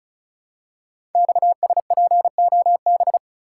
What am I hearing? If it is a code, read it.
Morse code, heard as a keyed tone, about 36 wpm, 700 Hz. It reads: XSPOB
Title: Code Groups - 5 Letters - 35wpm